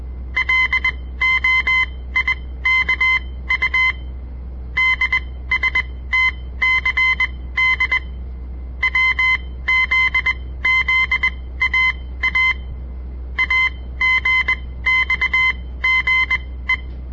Morse code, heard as a keyed tone, about 20 wpm, spaced as for 15 wpm, 2050 Hz. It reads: LOIKU DSTCD WZZAA AGXGE
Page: On the air Morse code practice from the Southern Montana Amatuer Radio Association; all code, no read back